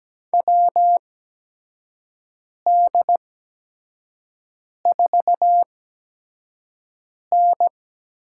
Morse code, heard as a keyed tone, about 17 wpm, spaced as for 5 wpm, 700 Hz. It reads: WD4N
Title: USA Call Signs - 6 Characters - 8x Character Spacing - 17wpm